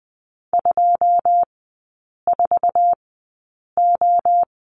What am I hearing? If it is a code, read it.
Morse code, heard as a keyed tone, about 20 wpm, spaced as for 8 wpm, 690 Hz. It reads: 24O
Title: 133 Lesson 22e - Question Mark - Three Characters 20wpm